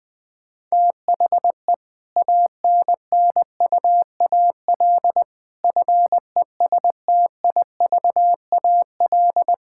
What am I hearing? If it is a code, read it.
Morse code, heard as a keyed tone, about 20 wpm, 700 Hz. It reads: THE ANNUAL FESTIVAL
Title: Difficult Sentences with Hints - 20wpm